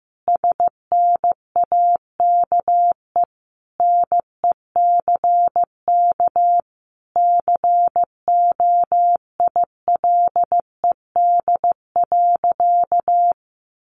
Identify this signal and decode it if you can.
Morse code, heard as a keyed tone, about 14 wpm, 700 Hz. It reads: SNAKE NECK COILED.
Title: A Sound Of Thunder 15wpm